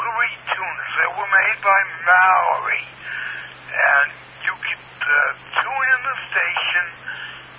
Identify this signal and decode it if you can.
Emergency Alert System (SAME) tones (header bursts), heard as harsh buzzing bursts, with speech. It is heard from 0.7 to 7.2 s.